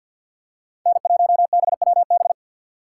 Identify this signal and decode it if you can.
Morse code, heard as a keyed tone, about 47 wpm, 700 Hz. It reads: N1BWB
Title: USA Call Signs - 5 Characters - 3x Character Spacing - 50wpm